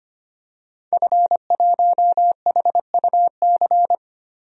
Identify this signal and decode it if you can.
Morse code, heard as a keyed tone, about 25 wpm, 700 Hz. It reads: F1HUC
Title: DXCC-100 Call Signs - 5 Characters - Repeated 3x - 25wpm